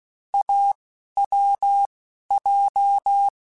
Morse code, heard as a keyed tone, about 16 wpm, spaced as for 11 wpm, 800 Hz. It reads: AWJ
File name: Short Runs / Short Runs-mp3s / awj1ndb6gz7 @10